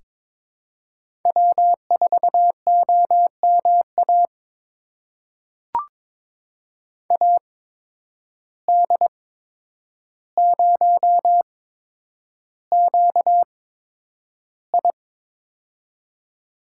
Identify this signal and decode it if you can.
Morse code, heard as a keyed tone, about 22 wpm, spaced as for 9 wpm, 700 Hz. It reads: W4OMA AD0QI
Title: USA Call Signs - 8x Character Spacing - 22wpm